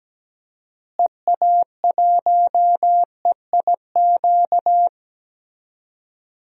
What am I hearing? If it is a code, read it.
Morse code, heard as a keyed tone, about 17 wpm, 700 Hz. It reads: EA1EIQ
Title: DXCC-100 Call Signs - 6 Characters - Repeated 3x - 17wpm